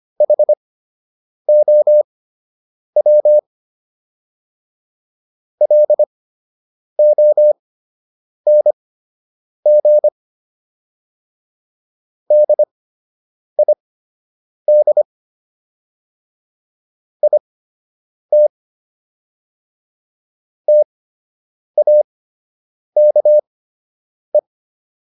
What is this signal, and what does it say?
Morse code, heard as a keyed tone, about 25 wpm, spaced as for 8 wpm, 600 Hz. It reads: HOW LONG DID IT TAKE